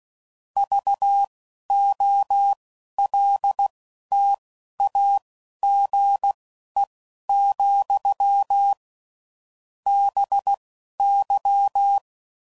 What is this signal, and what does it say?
Morse code, heard as a keyed tone, about 16 wpm, spaced as for 11 wpm, 800 Hz. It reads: VOLTAGE, BY
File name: CW Code Groups @10WPM / Technical Subjects / XC